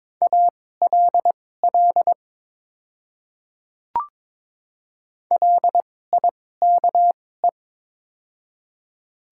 Morse code, heard as a keyed tone, about 22 wpm, spaced as for 16 wpm, 690 Hz. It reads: ALL LIKE
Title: 2x Character Spacing - Top 100 Words - 22wpm